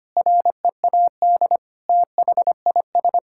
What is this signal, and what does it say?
Morse code, heard as a keyed tone, about 25 wpm, 700 Hz. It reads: READ THIS
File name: practice / 25wpm / Sentences with 9-Word Phrases - 25wpm